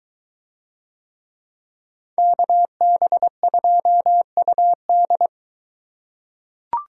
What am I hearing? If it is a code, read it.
Morse code, heard as a keyed tone, about 21 wpm, 700 Hz. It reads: KB2UD